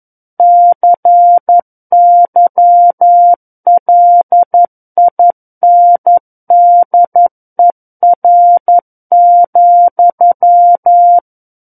Morse code, heard as a keyed tone, about 11 wpm, 700 Hz. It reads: CYLINDER,
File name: 11 WPM